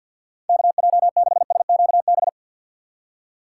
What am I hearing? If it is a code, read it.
Morse code, heard as a keyed tone, about 52 wpm, 700 Hz. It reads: KJ6SQB
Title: USA Call Signs - 6 Characters - 4x Character Spacing - 50wpm